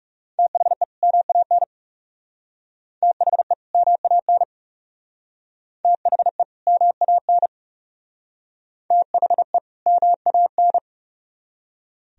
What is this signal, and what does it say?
Morse code, heard as a keyed tone, about 38 wpm, 700 Hz. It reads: THE MAN THE MAN THE MAN THE MAN